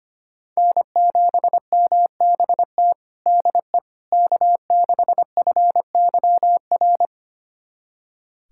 Morse code, heard as a keyed tone, about 25 wpm, 700 Hz. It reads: N7MBT DE K6FYR
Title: USA Callsign DE Callsign - 5 Characters - 25wpm